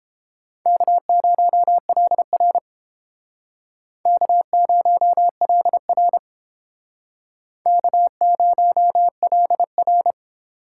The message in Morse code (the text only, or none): K0LR K0LR K0LR